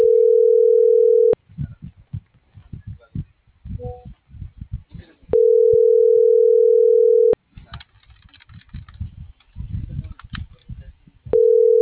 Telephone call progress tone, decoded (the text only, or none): ringback tone